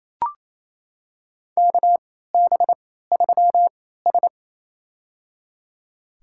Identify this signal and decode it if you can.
Morse code, heard as a keyed tone, about 28 wpm, spaced as for 16 wpm, 700 Hz. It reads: KB3S
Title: USA Call Signs - 4 Characters - 3x Character Spacing - 28wpm